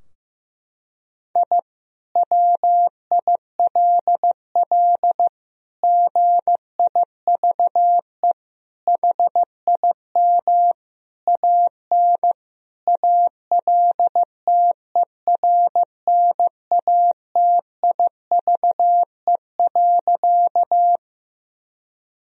Morse code, heard as a keyed tone, about 15 wpm, 700 Hz. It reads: I WILL GIVE HIM AN ALTERNATIVE.
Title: Sentences from Top 1000 Words - Review Part 4 - 15wpm